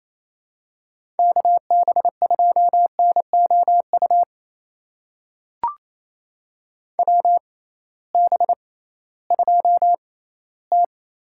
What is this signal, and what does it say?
Morse code, heard as a keyed tone, about 28 wpm, spaced as for 16 wpm, 700 Hz. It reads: KB2NOU WB2T
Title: USA Call Signs - 6 Characters - 6x Character Spacing - 28wpm